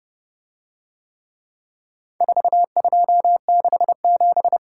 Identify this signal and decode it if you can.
Morse code, heard as a keyed tone, about 29 wpm, 700 Hz. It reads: 4267